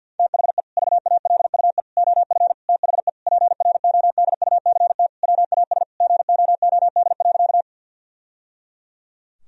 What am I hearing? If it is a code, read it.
Morse code, heard as a keyed tone, about 50 wpm, 700 Hz. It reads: THE VALUE OF THE PRODUCT WAS GOOD.
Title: Sentences from Top 1000 Words - Review Part 3 - 50wpm